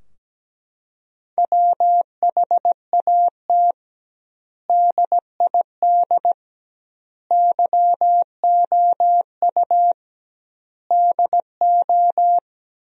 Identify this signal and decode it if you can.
Morse code, heard as a keyed tone, about 17 wpm, 700 Hz. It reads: WHAT DID YOU DO
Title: 2x Word-Spacing - Sentences from Top 100 Words - 17wpm